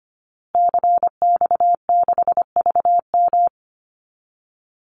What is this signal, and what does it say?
Morse code, heard as a keyed tone, about 25 wpm, 700 Hz. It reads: CX6VM
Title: Call Signs Repeated 2x - 25wpm